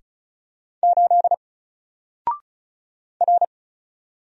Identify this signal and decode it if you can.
Morse code, heard as a keyed tone, about 34 wpm, 700 Hz. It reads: 8 R
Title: Single Letter-Number Alt 35wpm